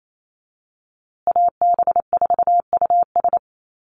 Morse code, heard as a keyed tone, about 28 wpm, 700 Hz. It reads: AB4US
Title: USA Call Signs Repeated 2x 28wpm